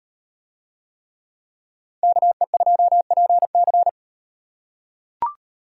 Morse code, heard as a keyed tone, about 38 wpm, 700 Hz. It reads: KE2PC